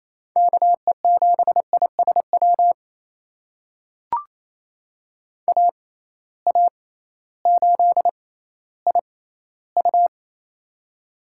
Morse code, heard as a keyed tone, about 28 wpm, spaced as for 15 wpm, 700 Hz. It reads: KE7ISW AA8IU
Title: USA Call Signs - 6x Character Spacing - 28wpm